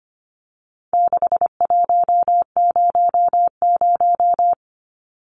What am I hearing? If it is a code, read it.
Morse code, heard as a keyed tone, about 25 wpm, 700 Hz. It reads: 6100
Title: Numbers - 4-Digit 25wpm